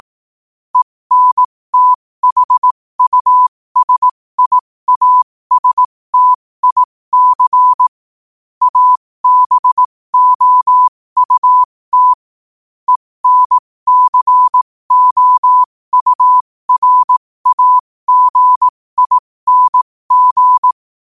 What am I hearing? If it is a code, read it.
Morse code, heard as a keyed tone, about 18 wpm, spaced as for 15 wpm, 1000 Hz. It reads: ENTHUSIASTIC ABOUT ENCOURAGING
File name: qotdfiles / 15wpm